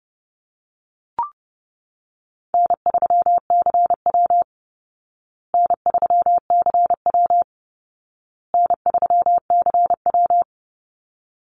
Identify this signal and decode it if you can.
Morse code, heard as a keyed tone, about 28 wpm, 700 Hz. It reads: N3CW N3CW N3CW
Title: Easy Call Signs Repeated 3x - 30wpm